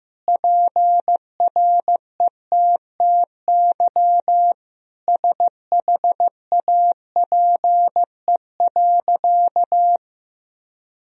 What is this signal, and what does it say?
Morse code, heard as a keyed tone, about 15 wpm, 700 Hz. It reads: PRETTY SHAPE.
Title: Sentences from Top 1000 Words - Review Part 3 - 15wpm